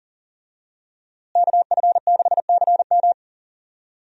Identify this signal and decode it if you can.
Morse code, heard as a keyed tone, about 40 wpm, 700 Hz. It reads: KF6CM
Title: USA Call Signs - 5 Characters - Repeated 2x - 40wpm